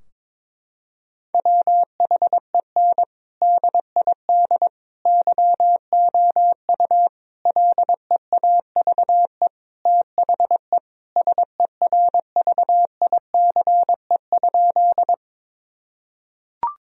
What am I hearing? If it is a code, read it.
Morse code, heard as a keyed tone, about 22 wpm, 700 Hz. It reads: WHEN DID YOU LEAVE THE SERVICE?
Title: Sentences from Top 1000 Words - Review Part 4 - 22wpm